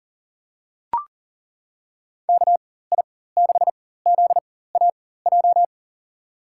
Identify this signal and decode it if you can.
Morse code, heard as a keyed tone, about 40 wpm, spaced as for 19 wpm, 700 Hz. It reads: KI6ZAJ
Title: USA Call Signs - 6 Characters - 4x Character Spacing - 40wpm